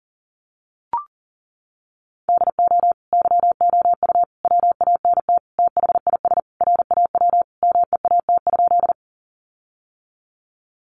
Morse code, heard as a keyed tone, about 40 wpm, 700 Hz. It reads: DO YOU WANT THIS RAW MEAT?
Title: Sentences from Top 1500 Words 40wpm